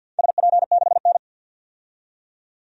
Morse code, heard as a keyed tone, about 52 wpm, 700 Hz. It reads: SP6N
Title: DXCC-100 Call Signs - 4 Characters - 50wpm